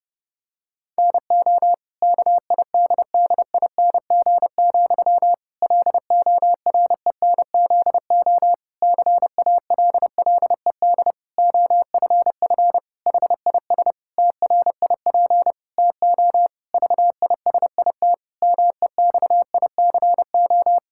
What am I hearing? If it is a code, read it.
Morse code, heard as a keyed tone, about 32 wpm, 700 Hz. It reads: NO KIDDING, LORENZO CALLED OFF HIS TRIP TO VISIT MEXICO
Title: Pangram Challenge v2 - 30wpm